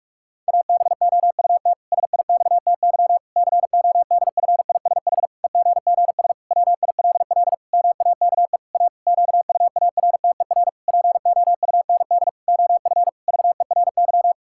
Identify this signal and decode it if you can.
Morse code, heard as a keyed tone, about 46 wpm, 700 Hz. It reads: ABOUT SIXTY CODFISH EGGS WILL MAKE A QUARTER POUND OF VERY
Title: Pangram Challenge v2 - 45wpm